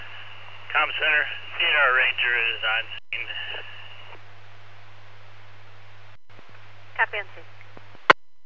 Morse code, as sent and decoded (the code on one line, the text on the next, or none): none